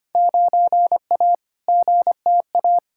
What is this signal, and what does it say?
Morse code, heard as a keyed tone, about 25 wpm, 700 Hz. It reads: 9A GTA
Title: Field Day Exchanges Encore - 25wpm